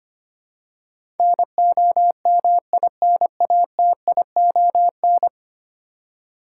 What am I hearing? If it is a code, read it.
Morse code, heard as a keyed tone, about 25 wpm, 700 Hz. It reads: NOMINATION
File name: 10 Letter Words - First 3 Letters Given - 25wpm